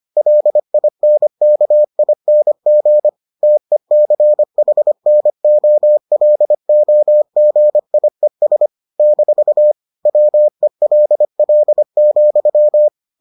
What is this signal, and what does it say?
Morse code, heard as a keyed tone, about 25 wpm, 600 Hz. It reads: LINKING TECHNOLOGIES - WELL,